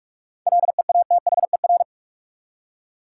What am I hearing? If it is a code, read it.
Morse code, heard as a keyed tone, about 47 wpm, 700 Hz. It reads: LEATHER
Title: Seven Letter Words - 45wpm